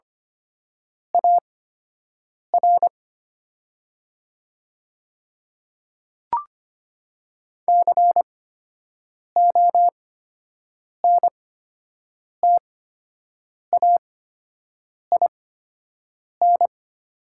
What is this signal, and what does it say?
Morse code, heard as a keyed tone, about 25 wpm, spaced as for 7 wpm, 700 Hz. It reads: AR CONTAIN
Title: 8x Character Spacing - Top 500 Words - 25wpm